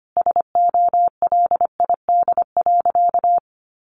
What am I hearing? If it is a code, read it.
Morse code, heard as a keyed tone, about 25 wpm, 700 Hz. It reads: SOLID.
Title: Sentences from Top 1500 Words 25wpm